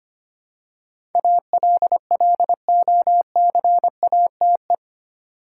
Morse code, heard as a keyed tone, about 25 wpm, 700 Hz. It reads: ALLOCATE
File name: Eight Letter Words - 25wpm